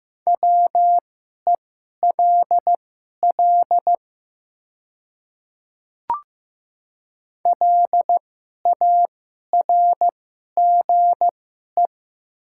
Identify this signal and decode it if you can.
Morse code, heard as a keyed tone, about 15 wpm, spaced as for 11 wpm, 700 Hz. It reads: WELL LARGE
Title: 2x Character Spacing - Top 200 Words - 15wpm